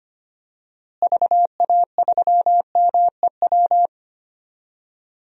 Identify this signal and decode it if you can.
Morse code, heard as a keyed tone, about 25 wpm, 700 Hz. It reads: VA3MEW
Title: DXCC-100 Call Signs - 6 Characters - Repeated 3x - 25wpm